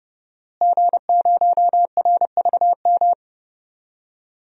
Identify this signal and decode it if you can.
Morse code, heard as a keyed tone, about 30 wpm, 700 Hz. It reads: G0RVM
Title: DXCC-100 Call Signs - 5 Characters - Repeated 3x - 30wpm